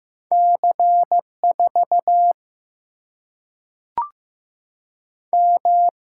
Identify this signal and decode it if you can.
Morse code, heard as a keyed tone, about 15 wpm, 700 Hz. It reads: C 4 M
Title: Single-Letter Single-Number 15wpm